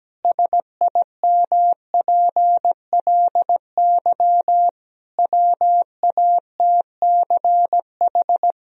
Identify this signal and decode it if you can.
Morse code, heard as a keyed tone, about 17 wpm, 700 Hz. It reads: SIMPLY WATCH
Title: Sentences from Top 1000 Words - Review Part 4 - 17wpm